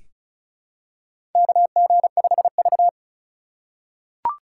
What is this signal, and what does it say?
Morse code, heard as a keyed tone, about 34 wpm, 700 Hz. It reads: KG5V
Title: K1USN SST QSO Practice - 35wpm